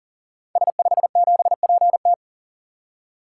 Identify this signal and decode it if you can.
Morse code, heard as a keyed tone, about 40 wpm, 690 Hz. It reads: S57PT
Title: Code Groups - 5 Letter:Numbers - 40wpm